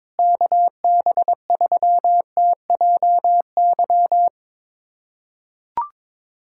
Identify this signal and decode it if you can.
Morse code, heard as a keyed tone, about 22 wpm, 700 Hz. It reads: KB3TJY
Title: USA Callsign DE Callsign - 6 Characters - 22wpm